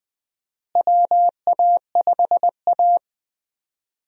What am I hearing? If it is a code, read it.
Morse code, heard as a keyed tone, about 20 wpm, 700 Hz. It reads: WA5A
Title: USA Call Signs - 4 Characters - Repeated 2x - 20wpm